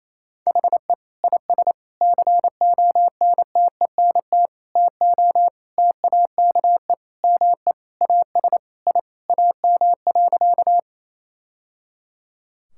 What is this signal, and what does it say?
Morse code, heard as a keyed tone, about 28 wpm, 700 Hz. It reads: HE IS CONTENT TO TAKE ME AS I AM.
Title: Sentences from Top 1000 Words - Review Part 6 - 28wpm